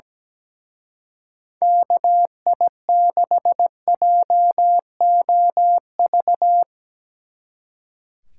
Morse code, heard as a keyed tone, about 17 wpm, 700 Hz. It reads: KI6JOV
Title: Speed Racing - USA Call Signs 17wpm